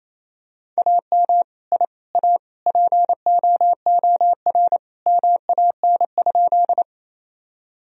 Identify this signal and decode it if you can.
Morse code, heard as a keyed tone, about 28 wpm, 700 Hz. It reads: AM I A POOR MAN?
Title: Sentences from Top 1000 Words - Review Part 2 - 28wpm